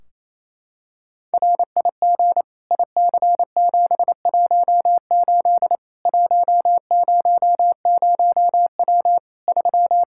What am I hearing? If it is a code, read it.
Morse code, heard as a keyed tone, about 28 wpm, 700 Hz. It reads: RIG IC718 100W 3
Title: Practice QSO - Exchange 2 - Info - 28wpm